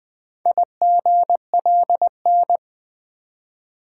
Morse code, heard as a keyed tone, about 20 wpm, 700 Hz. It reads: IGLN
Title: Four Letter-Number Combinations - 20wpm